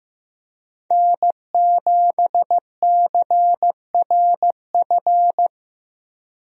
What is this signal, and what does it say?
Morse code, heard as a keyed tone, about 15 wpm, 700 Hz. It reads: N7CRF
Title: USA Call Signs - 5 Characters - Repeated 3x - 15wpm